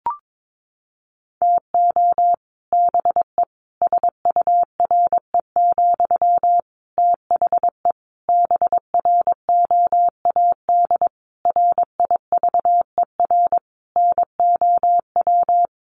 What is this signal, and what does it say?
Morse code, heard as a keyed tone, about 22 wpm, 700 Hz. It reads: TO BE SURE, THE BROAD RIVER NOW
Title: Sentences from Top 1000 Words 22wpm